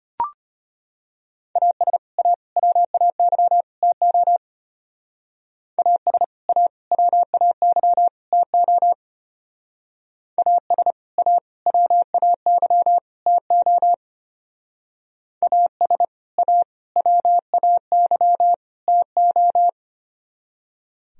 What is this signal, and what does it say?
Morse code, heard as a keyed tone, about 31 wpm, 700 Hz. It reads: AS A WAY TO AS A WAY TO AS A WAY TO AS A WAY TO
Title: Speed-Racing - Sets of 4 Words 25wpm